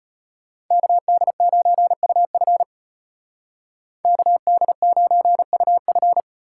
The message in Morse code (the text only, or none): KD9UF KD9UF